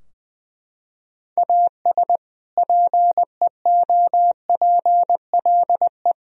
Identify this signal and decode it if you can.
Morse code, heard as a keyed tone, about 20 wpm, 700 Hz. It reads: AS PEOPLE
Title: Sentences from Top 1000 Words - Review Part 4 - 20wpm